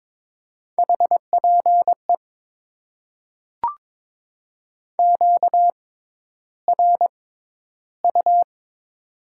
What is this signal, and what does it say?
Morse code, heard as a keyed tone, about 21 wpm, spaced as for 11 wpm, 700 Hz. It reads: HPE QRU
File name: Top 90 QSO Elements - 6x Character Spacing - 22wpm